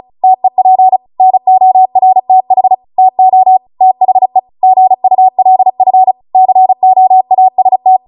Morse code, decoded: TEP NORTH TO THE GULF COAST